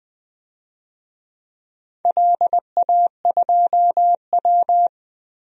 Morse code, heard as a keyed tone, about 20 wpm, 690 Hz. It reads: LA2W
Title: DXCC-100 Call Signs - 4 Characters - Repeated 3x - 20wpm